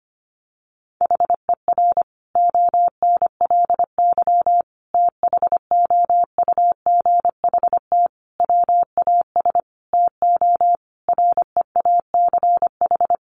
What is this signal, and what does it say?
Morse code, heard as a keyed tone, about 25 wpm, 700 Hz. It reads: HER ONLY THOUGHT WAS TO REACH
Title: Sentences from Top 400 Words 25wpm